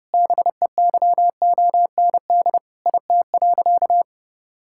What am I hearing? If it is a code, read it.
Morse code, heard as a keyed tone, about 30 wpm, 700 Hz. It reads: BEYOND IT.
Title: Sentences from Top 1000 Words - Review Part 6 - 30wpm